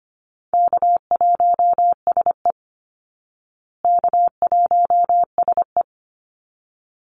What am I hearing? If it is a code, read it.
Morse code, heard as a keyed tone, about 25 wpm, 700 Hz. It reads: K1SE K1SE
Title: Easy Call Signs Repeated 2x - 25wpm